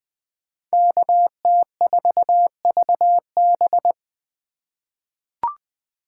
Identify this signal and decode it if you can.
Morse code, heard as a keyed tone, about 20 wpm, 700 Hz. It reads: KT4VB
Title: USA Call Signs - 3x Character Spacing - 20wpm